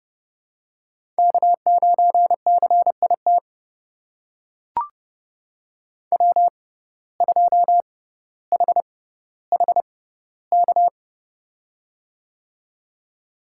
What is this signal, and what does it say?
Morse code, heard as a keyed tone, about 30 wpm, spaced as for 15 wpm, 700 Hz. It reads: K9CIT W2HHK
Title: USA Call Signs - 6x Character Spacing - 30wpm